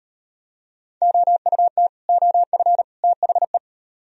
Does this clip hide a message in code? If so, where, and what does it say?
Yes: Morse code from 1.0 to 3.6 s, heard as a keyed tone, about 36 wpm, 700 Hz. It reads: OUT OF THE